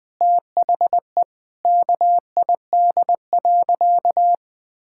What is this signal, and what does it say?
Morse code, heard as a keyed tone, about 21 wpm, 700 Hz. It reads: THE KID.
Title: Sentences from Top 1600 Words 20wpm